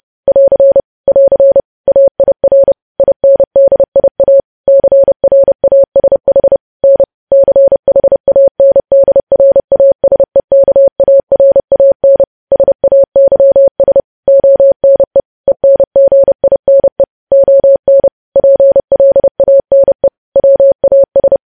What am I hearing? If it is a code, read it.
Morse code, heard as a keyed tone, about 30 wpm, 550 Hz. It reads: <AR> <AR> AIR INDIA CRASH N CHANDRASEKARAN SAYS ONE ENGINE ON PLANE WAS
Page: news headlines in morse code at 30 wpm